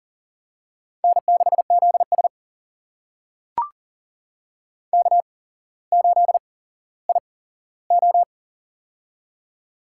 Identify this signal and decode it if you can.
Morse code, heard as a keyed tone, about 40 wpm, spaced as for 17 wpm, 700 Hz. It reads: N6ZS K8IO
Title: USA Call Signs - 4 Characters - 8x Character Spacing - 40wpm